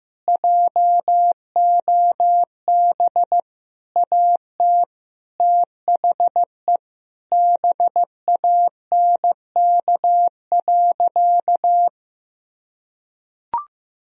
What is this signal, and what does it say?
Morse code, heard as a keyed tone, about 15 wpm, 700 Hz. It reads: JOB AT THE BANK.
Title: Sentences from Top 1000 Words - Review Part 4 - 15wpm